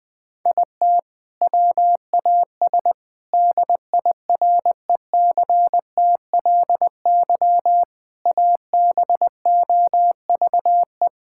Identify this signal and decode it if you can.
Morse code, heard as a keyed tone, about 21 wpm, 700 Hz. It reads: IT WAS DIRECTLY ABOVE
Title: Sentences from Top 1000 Words - Review Part 5 - 20wpm